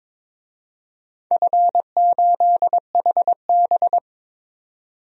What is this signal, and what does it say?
Morse code, heard as a keyed tone, about 22 wpm, 700 Hz. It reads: F8HB
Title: DXCC-100 Call Signs - 4 Characters - 22wpm